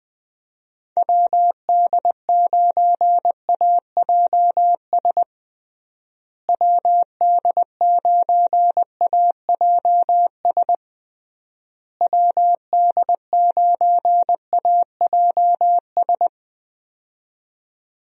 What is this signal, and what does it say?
Morse code, heard as a keyed tone, about 21 wpm, 700 Hz. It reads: WD9AJS WD9AJS WD9AJS
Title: USA Call Signs - 6 Characters - Repeated 3x - 20wpm